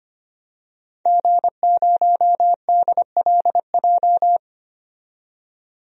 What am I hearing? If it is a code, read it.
Morse code, heard as a keyed tone, about 25 wpm, 700 Hz. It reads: G0DLJ
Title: DXCC-100 Call Signs - 5 Characters - 25wpm